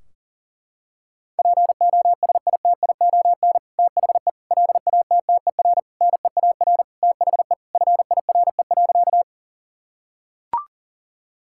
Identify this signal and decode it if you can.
Morse code, heard as a keyed tone, about 40 wpm, 700 Hz. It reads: POSITION THE LATTER NEAR THE FIRE.
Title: Sentences from Top 1600 Words 40wpm